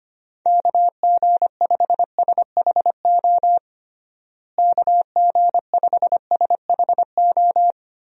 Morse code, heard as a keyed tone, about 25 wpm, 700 Hz. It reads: KG5SHO KG5SHO
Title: USA Call Signs - 6 Characters - Repeated 2x - 25wpm